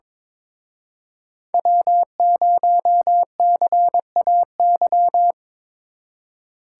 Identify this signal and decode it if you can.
Morse code, heard as a keyed tone, about 25 wpm, 700 Hz. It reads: W0CAY
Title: Speed Racing - USA Call Signs 22wpm